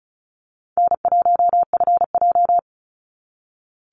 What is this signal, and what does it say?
Morse code, heard as a keyed tone, about 34 wpm, 700 Hz. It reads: N1FJ
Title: NA SOTA Activator Callsigns 35wpm